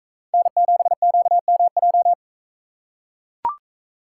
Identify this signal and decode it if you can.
Morse code, heard as a keyed tone, about 40 wpm, 700 Hz. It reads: N7QMJ